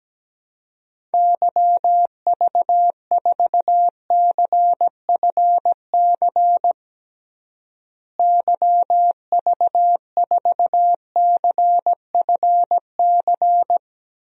Morse code, heard as a keyed tone, about 17 wpm, 700 Hz. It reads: YV4CFC YV4CFC
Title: DXCC-100 Call Signs - 6 Characters - Repeated 2x - 17wpm